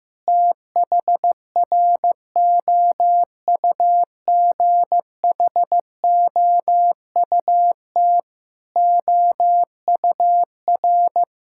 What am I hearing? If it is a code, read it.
Morse code, heard as a keyed tone, about 15 wpm, 700 Hz. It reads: THROUGHOUT OUR
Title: Sentences from Top 1000 Words - Review Part 5 - 15wpm